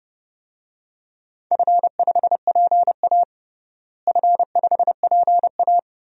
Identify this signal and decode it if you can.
Morse code, heard as a keyed tone, about 30 wpm, 700 Hz. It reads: F5PA F5PA
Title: DXCC-100 Call Signs - 4 Characters - Repeated 2x - 30wpm